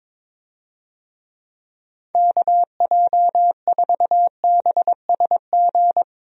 Morse code, heard as a keyed tone, about 21 wpm, 700 Hz. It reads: KJ4BSG
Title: USA Call Signs - 6 Characters - 22wpm